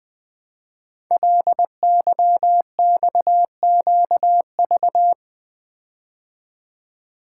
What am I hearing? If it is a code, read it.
Morse code, heard as a keyed tone, about 20 wpm, 700 Hz. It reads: LYXQV
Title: Code Groups - 5 Letters - 20wpm